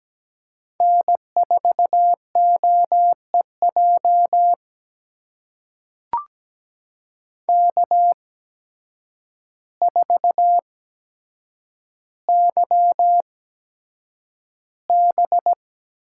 Morse code, heard as a keyed tone, about 17 wpm, spaced as for 8 wpm, 700 Hz. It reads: N4OEJ K4YB
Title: USA Call Signs - 8x Character Spacing - 17wpm